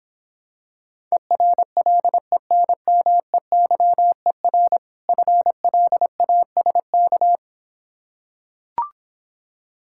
Morse code, heard as a keyed tone, about 26 wpm, 700 Hz. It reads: ERLENMEYER FLASK